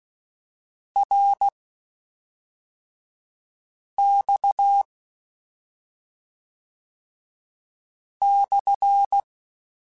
Morse code, heard as a keyed tone, about 16 wpm, spaced as for 3 wpm, 800 Hz. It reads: RX/
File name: Q Calls / Q Calls-mp3s / Q Calls SPSL1 @5